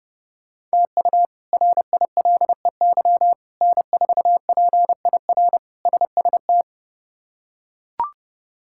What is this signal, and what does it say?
Morse code, heard as a keyed tone, about 30 wpm, 700 Hz. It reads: TU RILEY N4PIR SST
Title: K1USN SST QSO Practice - 30wpm